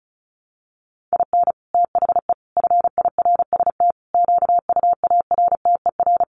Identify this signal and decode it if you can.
Morse code, heard as a keyed tone, about 34 wpm, 700 Hz. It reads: IN THE FIRST QUARTER
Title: Sets of 5 Words Encore2 35wpm